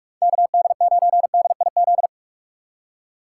Morse code, heard as a keyed tone, about 48 wpm, 700 Hz. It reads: KD9DIZ